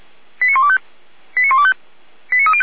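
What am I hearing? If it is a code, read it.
Five-tone selective call (ZVEI), heard as quick runs of five tones, tones of about 70 ms. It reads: 89125 89125 89305
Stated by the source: ZVEI five-tone selective call received by SDR, recorded by Feuerwehr Orlamünde fire brigade.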